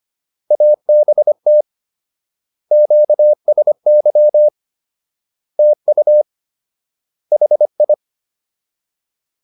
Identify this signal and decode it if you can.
Morse code, heard as a keyed tone, about 25 wpm, 600 Hz. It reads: ABT QSY TU HI